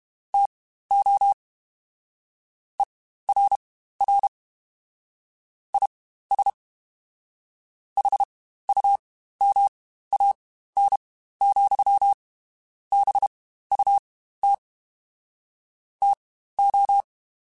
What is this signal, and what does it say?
Morse code, heard as a keyed tone, about 31 wpm, spaced as for 15 wpm, 800 Hz. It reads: TO ERR IS HUMAN, BUT TO
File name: Spaced / Chuckles Spaced-mp3s / HumanErr sp @30